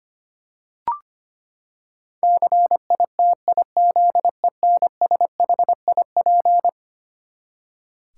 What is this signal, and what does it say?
Morse code, heard as a keyed tone, about 25 wpm, 700 Hz. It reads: CITIZENSHIP
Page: morse-code-ninja-practice-25wpm